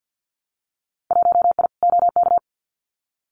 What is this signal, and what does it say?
Morse code, heard as a keyed tone, about 50 wpm, 700 Hz. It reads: 1I OK
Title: Winter Field Day Exchanges Encore 50wpm